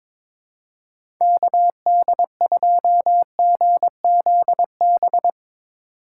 Morse code, heard as a keyed tone, about 22 wpm, 700 Hz. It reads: KD2GZB
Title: USA Call Signs - 6 Characters - 2x Character Spacing - 22wpm